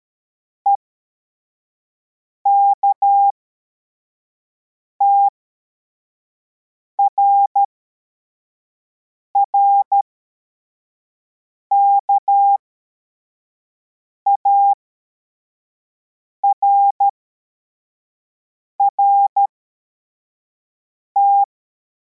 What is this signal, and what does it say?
Morse code, heard as a keyed tone, about 13 wpm, spaced as for 4 wpm, 800 Hz. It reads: EKTRR KARRT